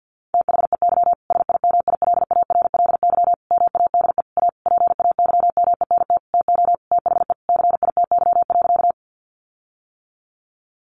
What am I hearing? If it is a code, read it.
Morse code, heard as a keyed tone, about 50 wpm, 700 Hz. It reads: THEY SIMILARLY MADE A PAYMENT TO THE CITY.
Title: Sentences from Top 1400 Words 50wpm